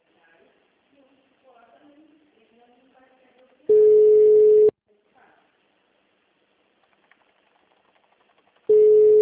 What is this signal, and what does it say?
Ringback tone (Europe, 425 Hz), with speech.
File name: out- -2004- - - .13